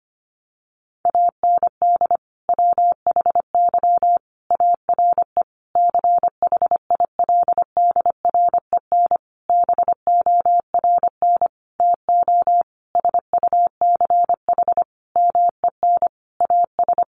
Morse code, heard as a keyed tone, about 25 wpm, 700 Hz. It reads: AND WHY ARE CHILDREN BORN TO SUCH MEN AS
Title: Sentences from Top 1000 Words 25wpm